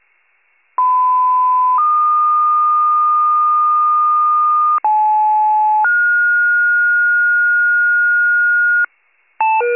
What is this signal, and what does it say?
Two-tone page, heard as a one-second tone, then a three-second tone, about 1010 then 1220 Hz, about 860 then 1430 Hz, with noise. It is heard from 0.8 to 8.8 s.